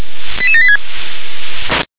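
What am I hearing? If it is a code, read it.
Five-tone selective call (ZVEI), heard as quick runs of five tones, tones of about 70 ms. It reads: 88786
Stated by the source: ZVEI five-tone selective call received on an SDR.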